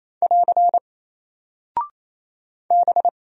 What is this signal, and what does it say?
Morse code, heard as a keyed tone, about 27 wpm, 700 Hz. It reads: <AR> B
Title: Single Character (Letter:Number:Punctuation:Prosign) 28wpm